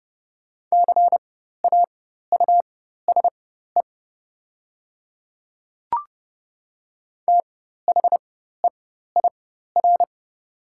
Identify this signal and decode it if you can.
Morse code, heard as a keyed tone, about 30 wpm, spaced as for 14 wpm, 700 Hz. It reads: CAUSE THEIR
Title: 4x Character Spacing - Top 200 Words - 30wpm